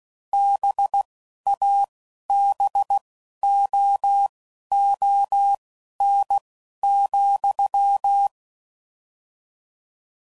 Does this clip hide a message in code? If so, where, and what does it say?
Yes: Morse code from 0.3 to 8.3 s, heard as a keyed tone, about 16 wpm, spaced as for 11 wpm, 800 Hz. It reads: BABOON,